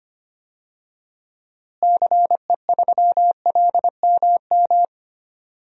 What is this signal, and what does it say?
Morse code, heard as a keyed tone, about 25 wpm, 700 Hz. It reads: CE3LMM
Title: DXCC-100 Call Signs - 6 Characters - Repeated 3x - 25wpm